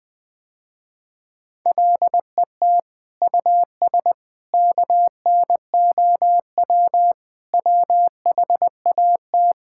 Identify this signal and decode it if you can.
Morse code, heard as a keyed tone, about 21 wpm, 700 Hz. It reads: LET US KNOW WHAT
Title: Sentences with 8-Word Phrases - Spoken Beginning - 20wpm